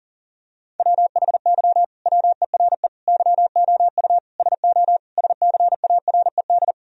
Morse code, heard as a keyed tone, about 40 wpm, 700 Hz. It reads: WHY WERE YOU SO SCARED
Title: Sentences from Top 1000 Words - Review Part 4 - 40wpm